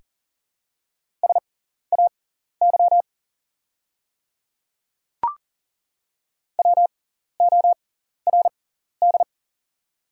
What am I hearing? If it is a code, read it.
Morse code, heard as a keyed tone, about 40 wpm, spaced as for 14 wpm, 700 Hz. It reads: SAY WORD
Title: 6x Character Spacing - Top 200 Words - 40wpm